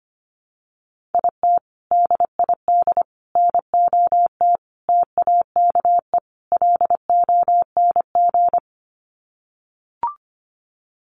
Morse code, heard as a keyed tone, about 25 wpm, 700 Hz. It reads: IT DID NOT TAKE LONG
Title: Sets of 5 Words Encore2 25wpm